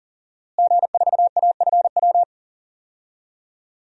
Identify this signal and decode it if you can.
Morse code, heard as a keyed tone, about 37 wpm, 700 Hz. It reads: G4AFW
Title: DXCC-100 Call Signs - 5 Characters - Repeated 3x - 40wpm